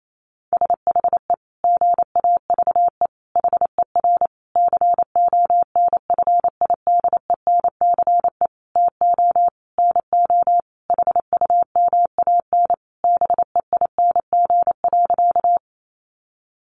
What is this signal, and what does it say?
Morse code, heard as a keyed tone, about 28 wpm, 700 Hz. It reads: SHE GAVE HER CONFIDENCE TO NO HUMAN BEING.
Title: Sentences from Top 1500 Words 28wpm